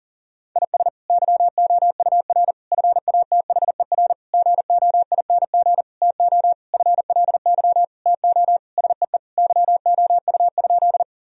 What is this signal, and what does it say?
Morse code, heard as a keyed tone, about 40 wpm, 700 Hz. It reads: IS YOUR FATHER GOING TO FLY TO SEE YOU?
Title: Sentences from Top 1000 Words - Review Part 2 - 40wpm